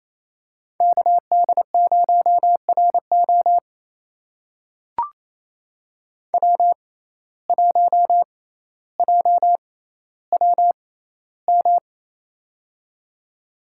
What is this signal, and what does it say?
Morse code, heard as a keyed tone, about 28 wpm, spaced as for 14 wpm, 700 Hz. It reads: KD0RO W1JWM
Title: USA Call Signs - 5 Characters - 6x Character Spacing - 28wpm